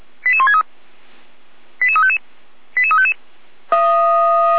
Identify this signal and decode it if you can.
Five-tone selective call (ZVEI), heard as quick runs of five tones, tones of about 70 ms. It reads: 88163 88355 88355
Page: SDR/ZVEI